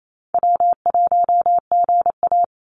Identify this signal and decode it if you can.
Morse code, heard as a keyed tone, about 28 wpm, 700 Hz. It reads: W1GA
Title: USA Call Signs 28wpm